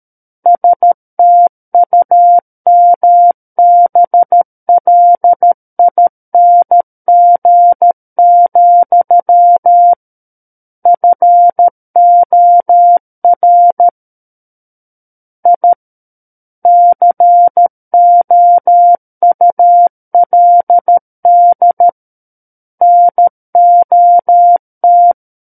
Morse code, heard as a keyed tone, about 13 wpm, 700 Hz. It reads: STUMBLING, FOR I COULD NOT
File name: Book One / 04-Chapter 4 - 13 WPM